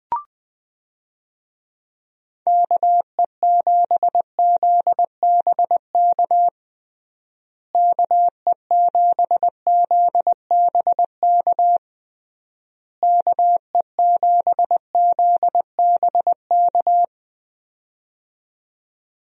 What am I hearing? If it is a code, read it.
Morse code, heard as a keyed tone, about 21 wpm, 700 Hz. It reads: KE7ZBK KE7ZBK KE7ZBK
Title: USA Call Signs - 6 Characters - Repeated 3x - 20wpm